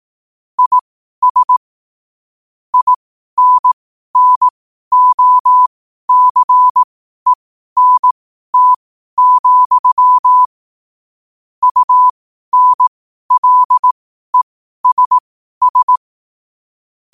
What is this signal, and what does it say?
Morse code, heard as a keyed tone, about 18 wpm, spaced as for 12 wpm, 1000 Hz. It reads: IS INNOCENT, UNLESS
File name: qotdfiles / 12wpm